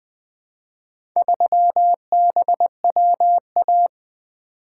Morse code, heard as a keyed tone, about 20 wpm, 700 Hz. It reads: 3BWA
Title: Four Letter-Number Combinations - 20wpm